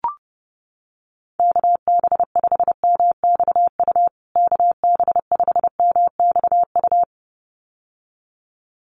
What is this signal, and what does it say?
Morse code, heard as a keyed tone, about 30 wpm, 700 Hz. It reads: KB5MXU KB5MXU
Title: USA Call Signs Repeated 2x 30wpm